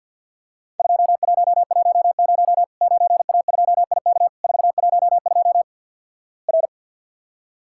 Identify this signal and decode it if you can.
Morse code, heard as a keyed tone, about 54 wpm, 700 Hz. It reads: 2110 9A2IK 412 R
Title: Worked All Europe DX Contest - QTC Exchange - 50wpm